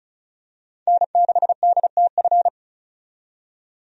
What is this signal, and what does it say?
Morse code, heard as a keyed tone, about 34 wpm, 700 Hz. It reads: N6DTF
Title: USA Call Signs - 5 Characters - 35wpm